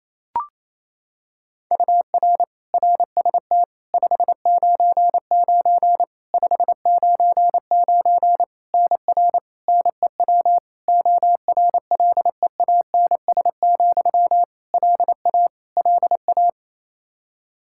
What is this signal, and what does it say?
Morse code, heard as a keyed tone, about 28 wpm, 700 Hz. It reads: UR RST 599 599 NR NEW ORLEANS, LA LA
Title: Practice QSO - Exchange 1 - Beginning - 28wpm